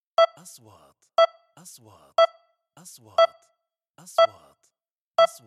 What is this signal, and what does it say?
Speech. There is no code.